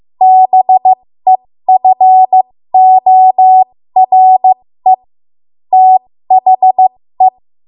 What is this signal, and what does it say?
Morse code, heard as a keyed tone, about 15 wpm, spaced as for 13 wpm, 750 Hz. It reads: BEFORE THE
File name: Morse / 13WPM